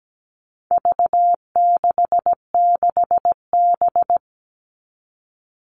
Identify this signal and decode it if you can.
Morse code, heard as a keyed tone, about 17 wpm, 700 Hz. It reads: V66B
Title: All Cut Numbers - 4-Digit 17wpm